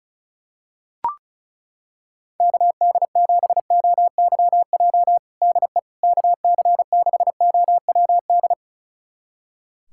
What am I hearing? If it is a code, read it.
Morse code, heard as a keyed tone, about 34 wpm, 700 Hz. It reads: KD7OYJ DE KC6OWD
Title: USA Callsign DE Callsign - 6 Characters - 35wpm